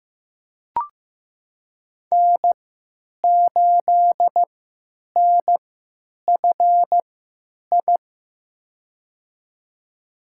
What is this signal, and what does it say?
Morse code, heard as a keyed tone, about 15 wpm, spaced as for 9 wpm, 700 Hz. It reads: N8NFI
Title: USA Call Signs - 5 Characters - 3x Character Spacing - 15wpm